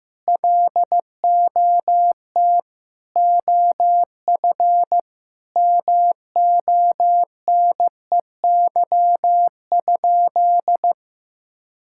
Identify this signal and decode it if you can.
Morse code, heard as a keyed tone, about 15 wpm, 700 Hz. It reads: LOT OF MONEY?
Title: Sentences from Top 1000 Words - Review Part 5 - 15wpm